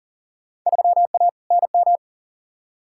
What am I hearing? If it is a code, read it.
Morse code, heard as a keyed tone, about 40 wpm, 700 Hz. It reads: 3A NM
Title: Field Day Exchanges Encore - 40wpm